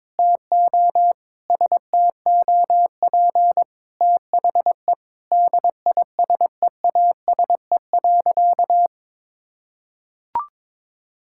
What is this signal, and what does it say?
Morse code, heard as a keyed tone, about 22 wpm, 700 Hz. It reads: TO STOP THE DISEASE.
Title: Sentences from Top 1000 Words - Review Part 6 - 22wpm